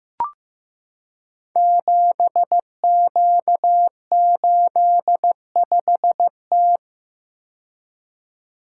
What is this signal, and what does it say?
Morse code, heard as a keyed tone, about 15 wpm, 700 Hz. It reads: 7Q85T
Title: Code Groups - 5 Letter:Numbers - 15wpm